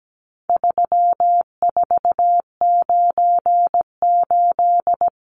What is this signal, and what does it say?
Morse code, heard as a keyed tone, about 17 wpm, 700 Hz. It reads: 3498
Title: Numbers - 4-Digit 17wpm